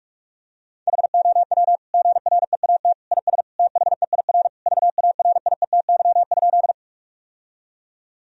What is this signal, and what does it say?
Morse code, heard as a keyed tone, about 41 wpm, 700 Hz. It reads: HOW GREAT IS THEIR VARIETY?
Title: Sentences from Top 1000 Words - Review Part 4 - 45wpm